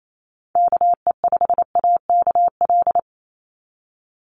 Morse code, heard as a keyed tone, about 28 wpm, 700 Hz. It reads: KE5AKL
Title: NA SOTA Chaser Callsigns 28wpm